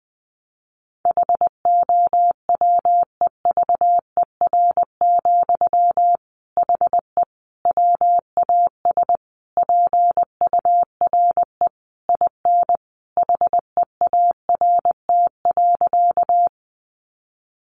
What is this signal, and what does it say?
Morse code, heard as a keyed tone, about 20 wpm, 700 Hz. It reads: HOWEVER, HE WAS PURE IN HEART.
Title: Sentences from Top 1500 Words 20wpm